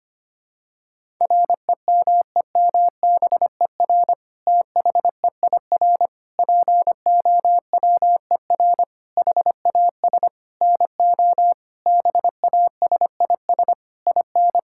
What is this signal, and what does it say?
Morse code, heard as a keyed tone, about 25 wpm, 700 Hz. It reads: REMEMBER THEIR POWER HAS NO BASIS IN
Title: Sentences from Top 1000 Words - Review Part 5 - 25wpm